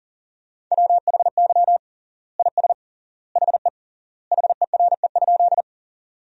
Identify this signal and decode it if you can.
Morse code, heard as a keyed tone, about 40 wpm, 700 Hz. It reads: WHY IS HE HERE?
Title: 3x Word-Spacing - Sentences from Top 200 Words - 40wpm